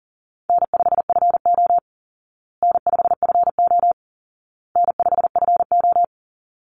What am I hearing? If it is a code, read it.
Morse code, heard as a keyed tone, about 40 wpm, 700 Hz. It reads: N5FO N5FO N5FO
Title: Easy Call Signs Repeated 3x - 40wpm